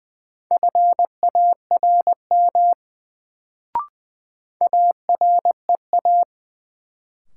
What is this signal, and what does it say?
Morse code, heard as a keyed tone, about 20 wpm, 700 Hz. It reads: FARM AREA
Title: Sentences from Top 1000 Words - Review Part 2 - New Word Drill - 20wpm